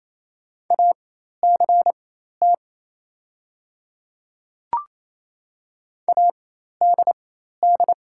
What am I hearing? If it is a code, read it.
Morse code, heard as a keyed tone, about 28 wpm, spaced as for 13 wpm, 700 Hz. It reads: ACT ADD
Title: 4x Character Spacing - Top 300 Words - 28wpm